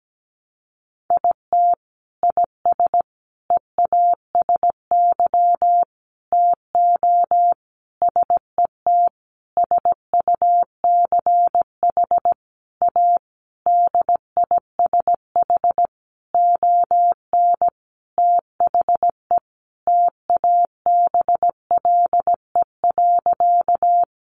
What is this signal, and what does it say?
Morse code, heard as a keyed tone, about 17 wpm, 700 Hz. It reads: IT IS EASY TO SET SUCH A DISH ON THE TABLE.
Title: Sentences from Top 1300 Words 17wpm